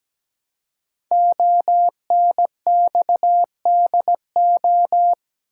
Morse code, heard as a keyed tone, about 17 wpm, 700 Hz. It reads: ONXDO
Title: Code Groups - 5 Letters - 17wpm